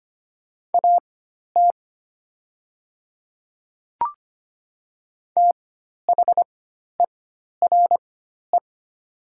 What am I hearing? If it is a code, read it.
Morse code, heard as a keyed tone, about 25 wpm, spaced as for 12 wpm, 700 Hz. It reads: AT THERE
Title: 4x Character Spacing - Top 100 Words - 25wpm